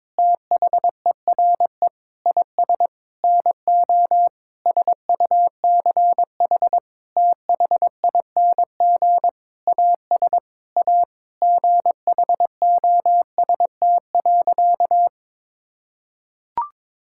Morse code, heard as a keyed tone, about 22 wpm, 700 Hz. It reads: THERE IS NO SUCH THING AS A GHOST.
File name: practice / 22wpm / Sentences with 8-Word Phrases - 22wpm